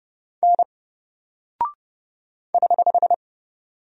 Morse code, heard as a keyed tone, about 30 wpm, 700 Hz. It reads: N <HH>
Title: Single Character (Letter:Number:Punctuation:Prosign) 30wpm